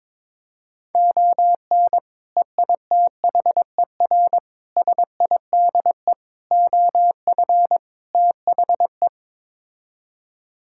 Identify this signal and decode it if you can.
Morse code, heard as a keyed tone, about 22 wpm, 700 Hz. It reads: ON EITHER SIDE OF THE
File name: practice / 22wpm / Build A Phrase - 5-Word Phrases - 22wpm